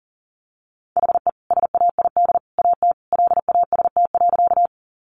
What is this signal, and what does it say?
Morse code, heard as a keyed tone, about 40 wpm, 700 Hz. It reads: HE SAID AT LAST.
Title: Sentences from Top 300 Words 40wpm